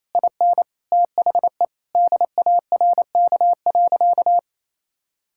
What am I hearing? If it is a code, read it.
Morse code, heard as a keyed tone, about 28 wpm, 700 Hz. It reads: IN THE DARK.
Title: Sentences from Top 1600 Words 28wpm